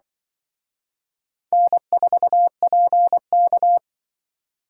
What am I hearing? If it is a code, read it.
Morse code, heard as a keyed tone, about 24 wpm, 700 Hz. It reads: N4PK
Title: Speed Racing - USA 1x2 Call Signs 20wpm